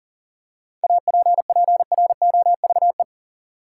Morse code, heard as a keyed tone, about 40 wpm, 700 Hz. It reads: APPROVE
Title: Seven Letter Words - 40wpm